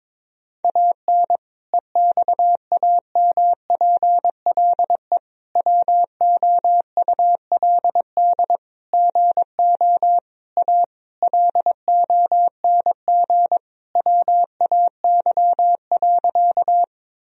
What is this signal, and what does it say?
Morse code, heard as a keyed tone, about 22 wpm, 700 Hz. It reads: AN EXAMPLE WOULD GO A LONG WAY.
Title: Sentences from Top 1000 Words - Review Part 1 - 22wpm